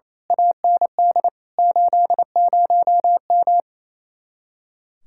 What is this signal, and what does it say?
Morse code, heard as a keyed tone, about 28 wpm, 700 Hz. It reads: AND 80M
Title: Speed-Racing Common QSO Phrases Encore - 28wpm